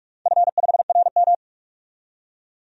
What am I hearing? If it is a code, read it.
Morse code, heard as a keyed tone, about 47 wpm, 700 Hz. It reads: F5RM
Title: DXCC-100 Call Signs - 4 Characters - 45wpm